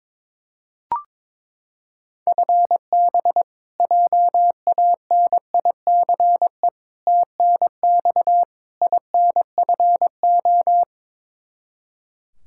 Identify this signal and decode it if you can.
Morse code, heard as a keyed tone, about 22 wpm, 700 Hz. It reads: FB JANICE TNX INFO
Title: K1USN SST QSO Practice 22wpm